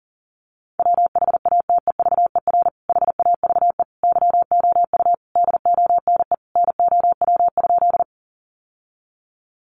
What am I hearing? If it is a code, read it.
Morse code, heard as a keyed tone, about 40 wpm, 700 Hz. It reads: WHATEVER HAVE YOU DONE NOW?
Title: Sentences from Top 1100 Words 40wpm